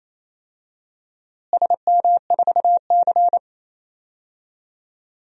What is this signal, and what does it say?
Morse code, heard as a keyed tone, about 28 wpm, 700 Hz. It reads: SM4C
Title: DXCC-100 Call Signs - 4 Characters - Repeated 2x - 28wpm